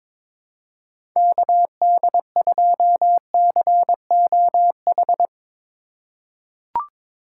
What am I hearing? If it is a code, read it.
Morse code, heard as a keyed tone, about 22 wpm, 700 Hz. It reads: KD2COH
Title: USA Call Signs - 6 Characters - 2x Character Spacing - 22wpm